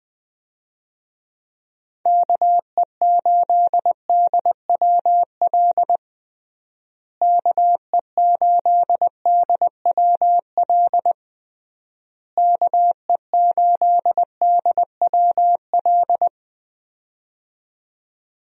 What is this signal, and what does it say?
Morse code, heard as a keyed tone, about 21 wpm, 700 Hz. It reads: KE8DWL KE8DWL KE8DWL
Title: USA Call Signs - 6 Characters - Repeated 3x - 20wpm